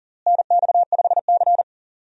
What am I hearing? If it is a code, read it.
Morse code, heard as a keyed tone, about 40 wpm, 700 Hz. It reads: NX5C
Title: USA Call Signs - 4 Characters - Repeated 2x - 40wpm